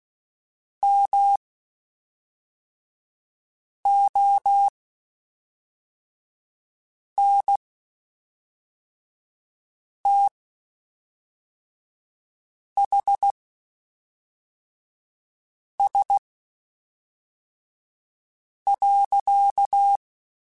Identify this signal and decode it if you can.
Morse code, heard as a keyed tone, about 16 wpm, spaced as for 3 wpm, 800 Hz. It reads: MONTHS.
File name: (spaced) / QSOs Spaced-mp3s / WD0BGP 1 @5 sp